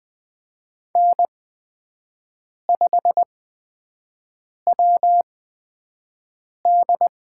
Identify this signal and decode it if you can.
Morse code, heard as a keyed tone, about 20 wpm, spaced as for 5 wpm, 700 Hz. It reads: N5WD
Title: USA Call Signs - 5 Characters - 8x Character Spacing - 20wpm